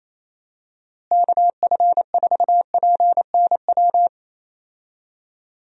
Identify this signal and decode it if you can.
Morse code, heard as a keyed tone, about 28 wpm, 700 Hz. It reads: KF4PNW
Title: USA Call Signs - 6 Characters - Repeated 2x - 28wpm